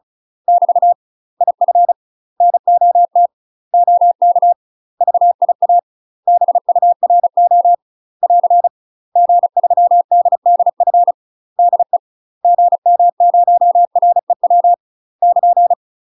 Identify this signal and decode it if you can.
Morse code, heard as a keyed tone, about 36 wpm, 700 Hz. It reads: <BT> IF NOT OK VIA BURO <AR> G3DDF DE GM0REW <KN>
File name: UK MASTER QSOs TRAINING / 35WPM / UK MASTER QSOs 0007